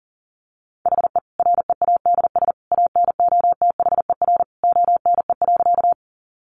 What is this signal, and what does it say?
Morse code, heard as a keyed tone, about 40 wpm, 700 Hz. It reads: HE READS ANOTHER ONE.
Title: Additional Easy Phrases 40wpm